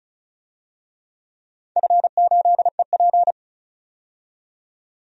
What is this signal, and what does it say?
Morse code, heard as a keyed tone, about 36 wpm, 700 Hz. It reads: F8EP
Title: DXCC-100 Call Signs - 4 Characters - 35wpm